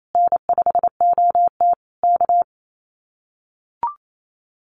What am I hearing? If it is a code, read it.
Morse code, heard as a keyed tone, about 28 wpm, 700 Hz. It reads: N5OT K
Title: Common QSO Phrases Encore 28wpm